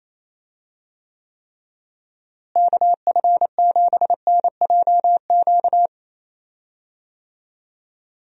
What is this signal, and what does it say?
Morse code, heard as a keyed tone, about 28 wpm, 700 Hz. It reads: KF7NJQ
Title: USA Call Signs - 6 Characters - Repeated 3x - 28wpm